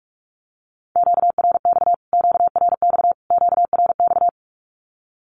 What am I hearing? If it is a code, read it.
Morse code, heard as a keyed tone, about 44 wpm, 700 Hz. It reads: QRX QRX QRX
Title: Q-Codes Repeated 3x 45wpm